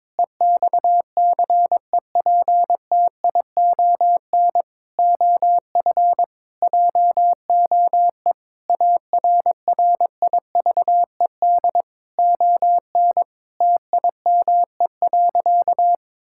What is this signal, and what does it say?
Morse code, heard as a keyed tone, about 22 wpm, 700 Hz. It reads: EXCEPTION OF JOE ARRIVED ON TIME.
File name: practice / 22wpm / Sentences with 4-Word Phrases - 22wpm